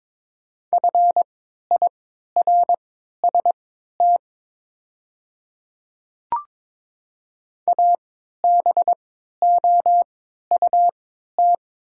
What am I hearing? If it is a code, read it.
Morse code, heard as a keyed tone, about 22 wpm, spaced as for 12 wpm, 700 Hz. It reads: FIRST ABOUT
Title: 3x Character Spacing - Top 100 Words - 22wpm